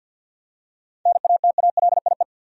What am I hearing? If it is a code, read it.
Morse code, heard as a keyed tone, about 45 wpm, 700 Hz. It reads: NATALIE
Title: USA Common Names - 50wpm